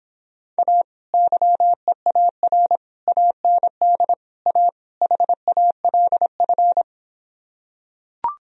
Morse code, heard as a keyed tone, about 26 wpm, 700 Hz. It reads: A YEAR AND A HALF